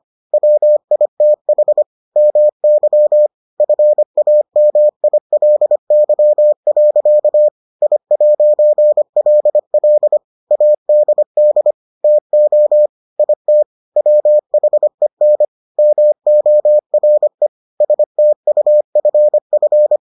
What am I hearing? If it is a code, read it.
Morse code, heard as a keyed tone, about 25 wpm, 600 Hz. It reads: WITH MY FAMILY. I'LL ADD TO IT WHEN MORE STUFF